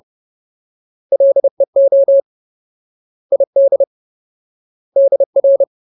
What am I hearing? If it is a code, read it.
Morse code, heard as a keyed tone, about 30 wpm, 550 Hz. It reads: LEO ID DR
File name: Ultimate Word List / 30WPM Ultimate